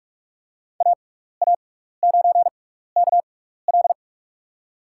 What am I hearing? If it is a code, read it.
Morse code, heard as a keyed tone, about 44 wpm, spaced as for 15 wpm, 700 Hz. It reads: AA9KL
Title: USA Call Signs - 5 Characters - 6x Character Spacing - 45wpm